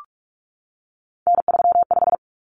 Morse code, heard as a keyed tone, about 44 wpm, 700 Hz. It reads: N35
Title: Common Cut Numbers - 3-Digit 45wpm